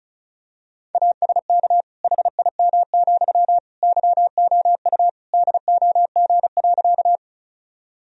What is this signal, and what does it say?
Morse code, heard as a keyed tone, about 36 wpm, 700 Hz. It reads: ASK HIM, YOU DOG.
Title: Sentences from Top 1000 Words - Review Part 2 - 35wpm